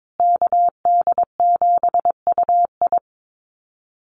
Morse code, heard as a keyed tone, about 22 wpm, 700 Hz. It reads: KD7UI
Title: Call Signs Repeated 2x - 22wpm